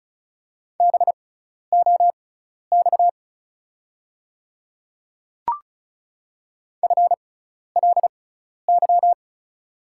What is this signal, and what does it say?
Morse code, heard as a keyed tone, about 36 wpm, spaced as for 12 wpm, 700 Hz. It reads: BOX FLY
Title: 6x Character Spacing - Top 500 Words - 35wpm